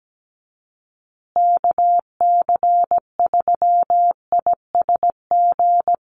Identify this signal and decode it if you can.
Morse code, heard as a keyed tone, about 17 wpm, 700 Hz. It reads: KC3ISG
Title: USA Call Signs Repeated 3x 17wpm